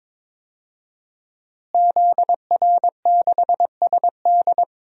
Morse code, heard as a keyed tone, about 22 wpm, 700 Hz. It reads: ZR6SD
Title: DXCC-100 Call Signs - 5 Characters - Repeated 3x - 22wpm